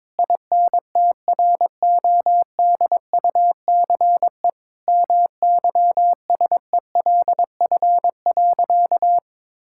Morse code, heard as a keyed tone, about 21 wpm, 700 Hz. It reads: INTRODUCE MYSELF.
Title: Sentences from Top 1600 Words 22wpm